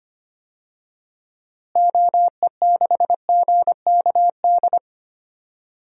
Morse code, heard as a keyed tone, about 25 wpm, 700 Hz. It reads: OE6GKD
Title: DXCC-100 Call Signs - 6 Characters - Repeated 2x - 25wpm